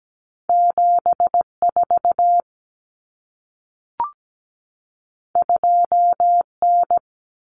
Morse code, heard as a keyed tone, about 17 wpm, 700 Hz. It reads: 74 2N
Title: Common Cut Numbers - 2-Digit 17wpm